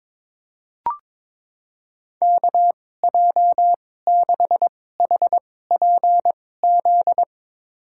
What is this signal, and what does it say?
Morse code, heard as a keyed tone, about 22 wpm, spaced as for 16 wpm, 700 Hz. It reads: KJ6HPZ
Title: USA Call Signs - 6 Characters - 2x Character Spacing - 22wpm